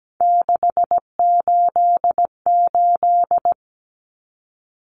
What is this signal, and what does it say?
Morse code, heard as a keyed tone, about 17 wpm, 700 Hz. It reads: 688
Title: Numbers - 3-Digit 17wpm